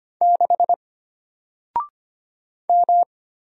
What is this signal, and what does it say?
Morse code, heard as a keyed tone, about 25 wpm, 700 Hz. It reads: 6 M
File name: Single-Letter-Number-25wpm